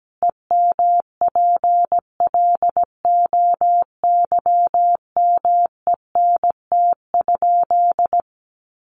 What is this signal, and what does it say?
Morse code, heard as a keyed tone, about 17 wpm, 700 Hz. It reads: EMPLOYMENT?
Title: Sentences from Top 1500 Words 17wpm